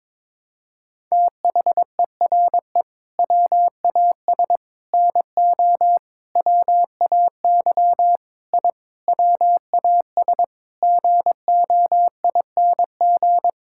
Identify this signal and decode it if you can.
Morse code, heard as a keyed tone, about 22 wpm, 700 Hz. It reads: THERE WAS NO WAY I WAS GOING
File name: practice / 22wpm / Sentences with 8-Word Phrases - 22wpm